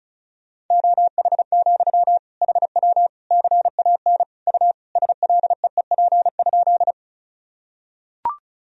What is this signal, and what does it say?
Morse code, heard as a keyed tone, about 34 wpm, 700 Hz. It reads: OH, HW CAN U SLEEP?
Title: Sentences from Top 500 Words and Abbreviations 35wpm